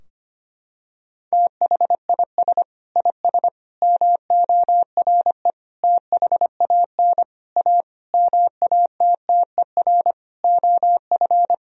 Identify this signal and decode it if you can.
Morse code, heard as a keyed tone, about 25 wpm, 700 Hz. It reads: THIS IS MORE THAN A MATTER OF
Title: Sentences from Top 1000 Words - Review Part 3 - 25wpm